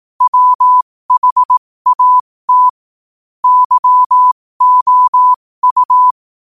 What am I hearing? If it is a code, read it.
Morse code, heard as a keyed tone, about 18 wpm, spaced as for 15 wpm, 1000 Hz. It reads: WHAT YOU
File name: qotdfiles / 15wpm